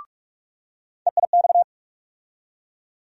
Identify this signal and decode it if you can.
Morse code, heard as a keyed tone, about 46 wpm, 700 Hz. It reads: EIX
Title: USA Call Sign Postfixes - 3 Letters 45wpm